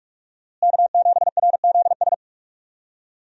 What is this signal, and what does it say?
Morse code, heard as a keyed tone, about 43 wpm, 700 Hz. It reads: K7RZS
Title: USA Call Signs - 5 Characters - Repeated 3x - 45wpm